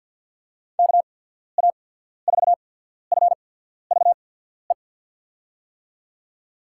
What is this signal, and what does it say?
Morse code, heard as a keyed tone, about 54 wpm, 700 Hz. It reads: K A 4 F V E
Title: USA Call Signs - 6 Characters - 8x Character Spacing - 50wpm